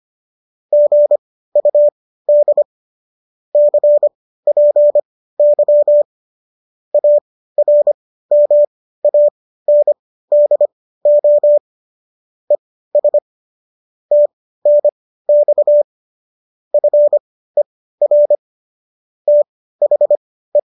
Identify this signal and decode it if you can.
Morse code, heard as a keyed tone, about 25 wpm, spaced as for 15 wpm, 590 Hz. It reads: GUD CPY ARMANDO ES TNX FER THE